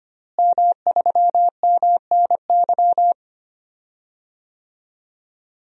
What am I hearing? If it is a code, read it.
Morse code, heard as a keyed tone, about 25 wpm, 700 Hz. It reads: M3MNY
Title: DXCC-100 Call Signs - 5 Characters - Repeated 3x - 25wpm